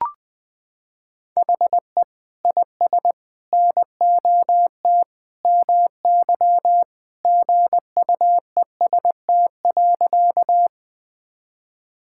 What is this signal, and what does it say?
Morse code, heard as a keyed tone, about 20 wpm, 700 Hz. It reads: HE IS NOT MY GUEST.
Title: Sentences from Top 1600 Words 20wpm